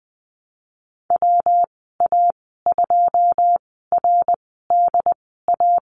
Morse code, heard as a keyed tone, about 20 wpm, spaced as for 15 wpm, 700 Hz. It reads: WA2RDA
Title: 128 Lesson 21g - Number 6 - Call Signs 20wpm